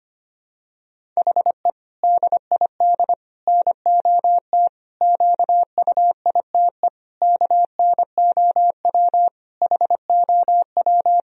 Morse code, heard as a keyed tone, about 25 wpm, 700 Hz. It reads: HE DID NOT QUITE KNOW HOW
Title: Sentences from Top 1000 Words - Review Part 1 - 25wpm